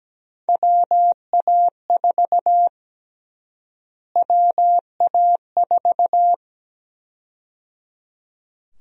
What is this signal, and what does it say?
Morse code, heard as a keyed tone, about 17 wpm, 700 Hz. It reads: WA4 WA4
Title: USA Call Sign Prefixes - Repeated 2x - 17wpm